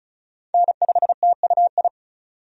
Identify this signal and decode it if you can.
Morse code, heard as a keyed tone, about 36 wpm, 700 Hz. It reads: N5TUI
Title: USA Call Signs - 5 Characters - 35wpm